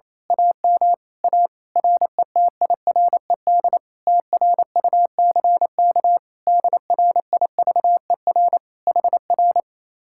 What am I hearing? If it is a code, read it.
Morse code, heard as a keyed tone, about 26 wpm, 700 Hz. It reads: AM A RETIRED TRUCK DRIVER HR
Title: Speed-Racing Common QSO Phrases Encore - 28wpm